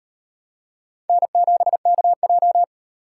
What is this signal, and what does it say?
Morse code, heard as a keyed tone, about 35 wpm, 700 Hz. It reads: N7KJ